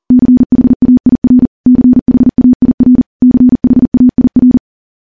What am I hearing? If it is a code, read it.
Morse code, heard as a keyed tone, about 40 wpm, 250 Hz. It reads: CHAIR CHAIR CHAIR
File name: morse / morse-training-40WPM